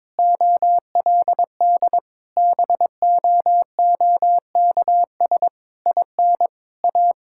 Morse code, heard as a keyed tone, about 22 wpm, 700 Hz. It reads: OLD BOOKS IN A
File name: Difficult Sentences with Hints - 22wpm